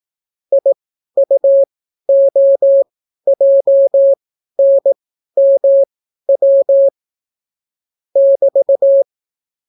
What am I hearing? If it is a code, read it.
Morse code, heard as a keyed tone, about 18 wpm, spaced as for 12 wpm, 550 Hz. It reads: IUOJNMW <BT>